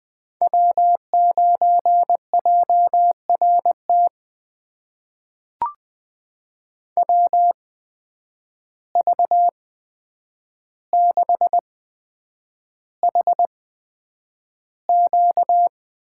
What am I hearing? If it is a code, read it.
Morse code, heard as a keyed tone, about 20 wpm, spaced as for 9 wpm, 700 Hz. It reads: W9JRT WV6HQ
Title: USA Call Signs - 5 Characters - 8x Character Spacing - 20wpm